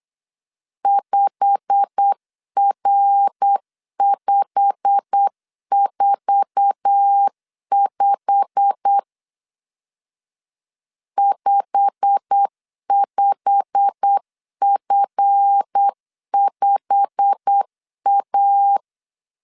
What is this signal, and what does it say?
Morse code, heard as a keyed tone, about 8 wpm, 800 Hz. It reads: 5R545 55F5A
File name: LEZIONI / 8 YJ45 / 8 YJ45 8wpm 3214